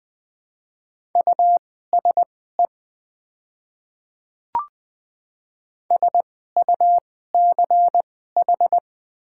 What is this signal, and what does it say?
Morse code, heard as a keyed tone, about 20 wpm, spaced as for 15 wpm, 700 Hz. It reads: USE SUCH
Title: 2x Character Spacing - Top 300 Words - 20wpm